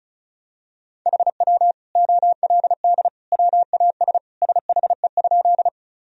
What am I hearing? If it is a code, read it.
Morse code, heard as a keyed tone, about 36 wpm, 700 Hz. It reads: HW OLD WAS SHE?
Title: Sentences from Top 200 Words and Abbreviations 35wpm